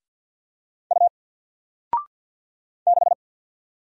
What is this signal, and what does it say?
Morse code, heard as a keyed tone, about 50 wpm, 700 Hz. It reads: U 6
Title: Single Letter-Number Alt 50wpm